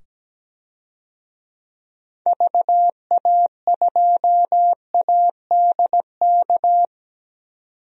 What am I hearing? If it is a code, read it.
Morse code, heard as a keyed tone, about 17 wpm, 700 Hz. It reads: VA2ADK
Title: DXCC-100 Call Signs - 6 Characters - 17wpm